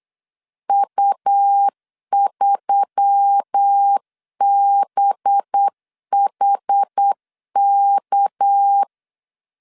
Morse code, heard as a keyed tone, about 8 wpm, 800 Hz. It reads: U3BHK
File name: CW / DET / 1 6 1500C S B 8wpm 5142